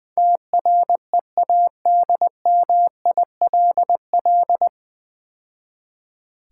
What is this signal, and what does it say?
Morse code, heard as a keyed tone, about 22 wpm, 700 Hz. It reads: TREADMILL
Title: Easy Words - Speed-Racing with Hints - 20wpm